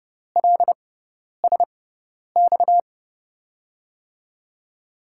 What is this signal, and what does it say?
Morse code, heard as a keyed tone, about 31 wpm, spaced as for 10 wpm, 700 Hz. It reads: LSX
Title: USA Call Signs - 6 Characters - 6x Character Spacing - 30wpm